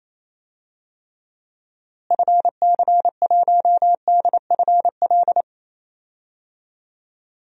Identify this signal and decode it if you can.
Morse code, heard as a keyed tone, about 28 wpm, 700 Hz. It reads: FC1DFL
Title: DXCC-100 Call Signs - 6 Characters - Repeated 2x - 28wpm